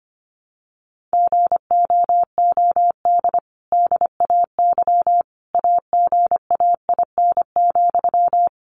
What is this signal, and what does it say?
Morse code, heard as a keyed tone, about 25 wpm, 700 Hz. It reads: GOOD DAY AGAIN,
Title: Sentences from Top 1100 Words 25wpm